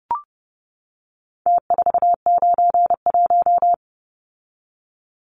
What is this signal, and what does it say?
Morse code, heard as a keyed tone, about 30 wpm, 700 Hz. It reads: T491
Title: Numbers - Contest Style - 4-Digit 30wpm